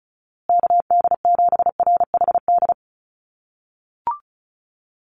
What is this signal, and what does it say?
Morse code, heard as a keyed tone, about 34 wpm, 700 Hz. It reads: KD7RHD
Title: USA Call Signs Repeated 3x 35wpm